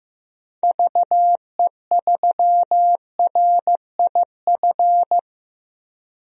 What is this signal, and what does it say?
Morse code, heard as a keyed tone, about 16 wpm, 700 Hz. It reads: VE3RIF
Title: DXCC-100 Call Signs - 6 Characters - Repeated 3x - 15wpm